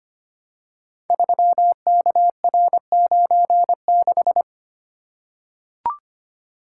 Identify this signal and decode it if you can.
Morse code, heard as a keyed tone, about 25 wpm, 700 Hz. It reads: 3KR96
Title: Code Groups - 5 Letter:Numbers - 25wpm